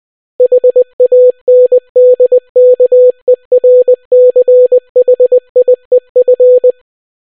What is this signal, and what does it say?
Morse code, heard as a keyed tone, about 20 wpm, 500 Hz. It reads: HANDKERCHIEF